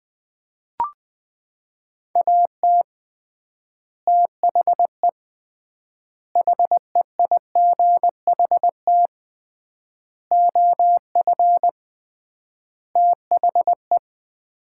Morse code, heard as a keyed tone, about 20 wpm, 700 Hz. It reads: AT THE HEIGHT OF THE
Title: 3x Word-Spacing - Sets of 5 Words - 20wpm